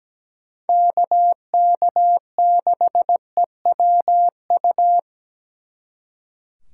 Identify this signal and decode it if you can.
Morse code, heard as a keyed tone, about 20 wpm, 700 Hz. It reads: KK6EWU
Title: Speed Racing - USA 2x3 Call Signs 17wpm